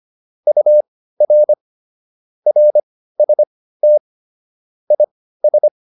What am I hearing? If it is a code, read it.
Morse code, heard as a keyed tone, about 25 wpm, spaced as for 15 wpm, 600 Hz. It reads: UR RST IS